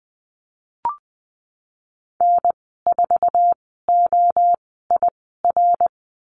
Morse code, heard as a keyed tone, about 20 wpm, spaced as for 15 wpm, 700 Hz. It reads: N4OIR
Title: 055 Lesson 10g - Letter r - Call Signs 20wpm